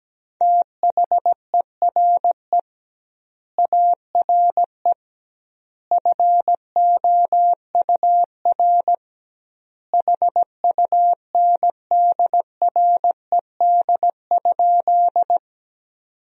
Morse code, heard as a keyed tone, about 17 wpm, 700 Hz. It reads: THERE ARE FOUR HUNDRED?
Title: 2x Word-Spacing - Sentences from Top 400 Words - 17wpm